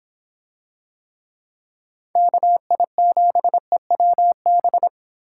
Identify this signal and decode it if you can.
Morse code, heard as a keyed tone, about 25 wpm, 700 Hz. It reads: KI7EWB